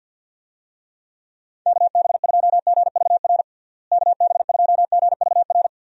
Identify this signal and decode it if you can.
Morse code, heard as a keyed tone, about 50 wpm, 700 Hz. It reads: KB2GVR KB2GVR
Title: USA Call Signs - 6 Characters - Repeated 3x - 50wpm